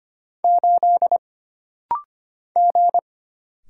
Morse code, heard as a keyed tone, about 25 wpm, 700 Hz. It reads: 8 G
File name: Single Character (Letter:Number:Punctuation:Prosign) - Rapid-Fire - 25wpm